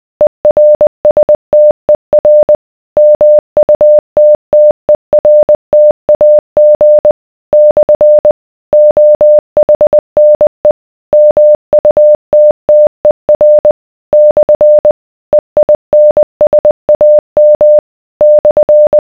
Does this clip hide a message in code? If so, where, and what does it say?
Yes: Morse code from 0.2 to 19.0 s, heard as a keyed tone, about 20 wpm, 600 Hz. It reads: ERSTER MUTTERTAG / OHNE MUTTER / EINSAM /